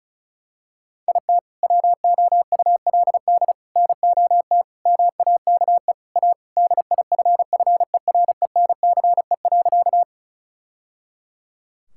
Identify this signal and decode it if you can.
Morse code, heard as a keyed tone, about 36 wpm, 700 Hz. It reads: IT WOULD NOT MAKE A DIFFERENCE.
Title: Sentences from Top 1000 Words - Review Part 5 - 35wpm